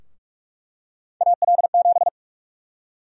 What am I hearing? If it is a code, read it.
Morse code, heard as a keyed tone, about 44 wpm, 700 Hz. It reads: AL7
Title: USA Call Sign Prefixes - Repeated 2x - 45wpm